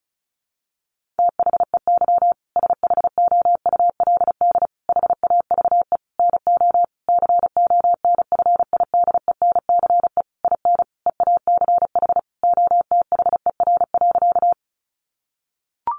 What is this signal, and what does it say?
Morse code, heard as a keyed tone, about 34 wpm, 700 Hz. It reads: THEY SHOULD HAVE NO CONFIDENCE IN EACH OTHER.
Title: Sentences from Top 1500 Words 35wpm